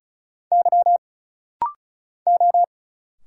Morse code, heard as a keyed tone, about 36 wpm, 700 Hz. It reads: Y O
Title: Single Letter-Number Alt - Rapid-Fire 35wpm